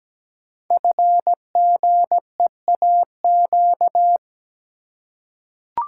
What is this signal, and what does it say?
Morse code, heard as a keyed tone, about 17 wpm, 700 Hz. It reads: FGEAQ
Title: Code Groups - 5 Letters - 17wpm